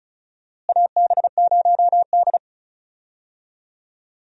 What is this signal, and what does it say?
Morse code, heard as a keyed tone, about 36 wpm, 700 Hz. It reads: AB0D
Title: USA Call Signs - 4 Characters - Repeated 2x - 35wpm